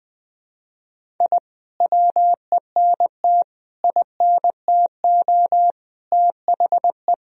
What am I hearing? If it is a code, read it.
Morse code, heard as a keyed tone, about 20 wpm, 700 Hz. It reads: I WENT INTO THE
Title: Sentences from Top 1000 Words - Review Part 1 - 20wpm